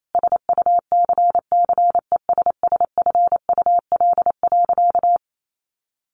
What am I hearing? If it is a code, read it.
Morse code, heard as a keyed tone, about 28 wpm, 700 Hz. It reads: SUCCESSFUL.
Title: Sentences from Top 1100 Words 28wpm